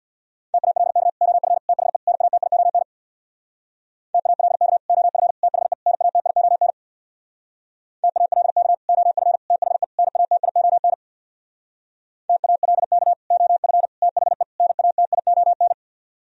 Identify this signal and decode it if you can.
Morse code, heard as a keyed tone, about 64 wpm, 700 Hz. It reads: TALK OF THE NATION TALK OF THE NATION TALK OF THE NATION TALK OF THE NATION